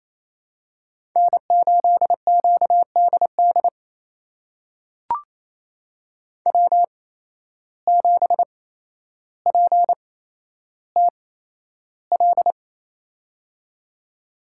Morse code, heard as a keyed tone, about 28 wpm, spaced as for 12 wpm, 700 Hz. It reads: N8QDD W7PTL
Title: USA Call Signs - 8x Character Spacing - 28wpm